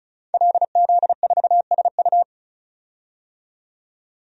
Morse code, heard as a keyed tone, about 34 wpm, 700 Hz. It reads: LZ4SU
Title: DXCC-100 Call Signs - 5 Characters - 35wpm